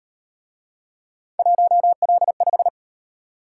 Morse code, heard as a keyed tone, about 38 wpm, 700 Hz. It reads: 1L5